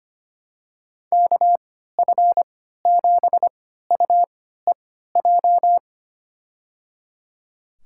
Morse code, heard as a keyed tone, about 25 wpm, spaced as for 14 wpm, 700 Hz. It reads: KF7UEJ
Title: USA Call Signs - 6 Characters - 3x Character Spacing - 25wpm